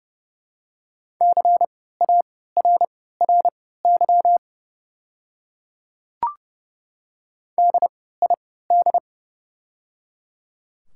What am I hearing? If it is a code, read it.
Morse code, heard as a keyed tone, about 30 wpm, spaced as for 17 wpm, 700 Hz. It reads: CARRY DID
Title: 3x Character Spacing - Top 500 Words - 30wpm